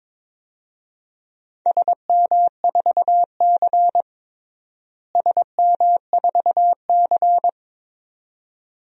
Morse code, heard as a keyed tone, about 22 wpm, 700 Hz. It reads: SM4C SM4C
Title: DXCC-100 Call Signs - 4 Characters - Repeated 2x - 22wpm